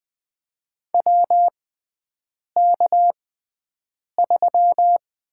Morse code, heard as a keyed tone, about 20 wpm, spaced as for 7 wpm, 700 Hz. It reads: WK3
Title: USA Call Signs - 4 Characters - 6x Character Spacing - 20wpm